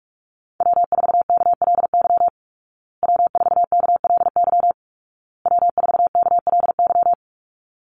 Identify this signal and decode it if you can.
Morse code, heard as a keyed tone, about 44 wpm, 700 Hz. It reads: W4KLY W4KLY W4KLY
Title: Call Signs Repeated 2x - 45wpm